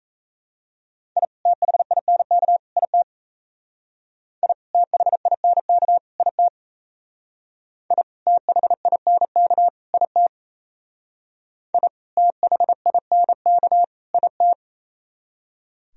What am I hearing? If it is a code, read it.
Morse code, heard as a keyed tone, about 36 wpm, 700 Hz. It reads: I THINK IT I THINK IT I THINK IT I THINK IT